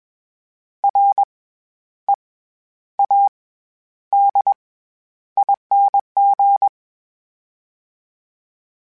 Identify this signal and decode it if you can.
Morse code, heard as a keyed tone, about 21 wpm, 800 Hz. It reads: R E A D ING
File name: Special INGs-mp3s / Special INGs-mp3s / SlowToFaster ing @20